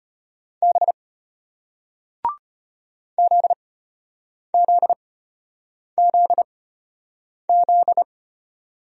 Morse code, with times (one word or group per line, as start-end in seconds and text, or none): 0.6-0.9 B
3.2-8.0 ZZZZ